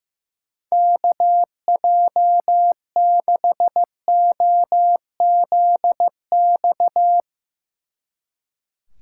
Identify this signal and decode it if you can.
Morse code, heard as a keyed tone, about 20 wpm, 700 Hz. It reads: KJ6OZX